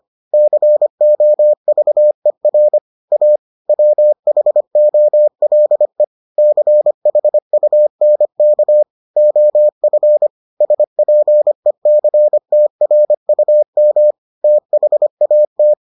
Morse code, heard as a keyed tone, about 25 wpm, 600 Hz. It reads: COVER A WHOLE CHUNK OF SPECTRUM THAT